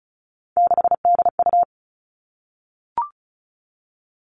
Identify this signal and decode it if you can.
Morse code, heard as a keyed tone, about 36 wpm, 700 Hz. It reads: 6DU
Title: Three Challenging Letter-Number Combinations 35wpm